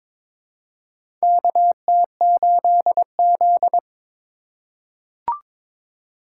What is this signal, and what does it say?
Morse code, heard as a keyed tone, about 22 wpm, 700 Hz. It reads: KT8Z
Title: USA Call Signs - 4 Characters - 3x Character Spacing - 22wpm